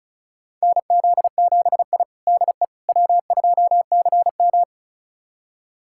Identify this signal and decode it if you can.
Morse code, heard as a keyed tone, about 36 wpm, 700 Hz. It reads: NZ7I DE W2CM
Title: USA Callsign DE Callsign - 4 Characters - 35wpm